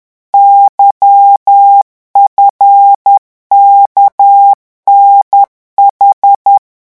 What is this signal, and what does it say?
Morse code, heard as a keyed tone, about 10 wpm, 800 Hz. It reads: YFKNH